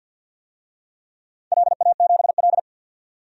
Morse code, heard as a keyed tone, about 52 wpm, 700 Hz. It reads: LA7L
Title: DXCC-100 Call Signs - 4 Characters - 50wpm